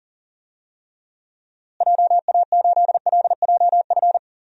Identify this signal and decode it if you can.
Morse code, heard as a keyed tone, about 37 wpm, 700 Hz. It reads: JA8LJF
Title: DXCC-100 Call Signs - 6 Characters - Repeated 3x - 40wpm